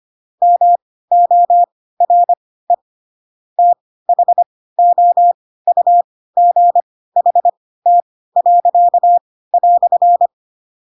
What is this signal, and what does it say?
Morse code, heard as a keyed tone, about 25 wpm, spaced as for 16 wpm, 700 Hz. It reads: MORE THOUGHT."
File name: Pinocchio / Pinocchio ch3 10 25 ik1hge